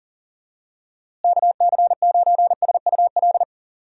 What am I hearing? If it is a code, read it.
Morse code, heard as a keyed tone, about 38 wpm, 700 Hz. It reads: KC9SUL
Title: USA Call Signs - 6 Characters - Repeated 2x - 40wpm